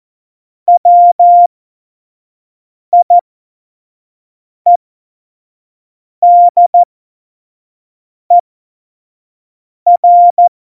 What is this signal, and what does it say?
Morse code, heard as a keyed tone, about 14 wpm, spaced as for 5 wpm, 700 Hz. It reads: WIEDER
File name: Morsekurs / Witze / Witz 28. August 2018, 25 F70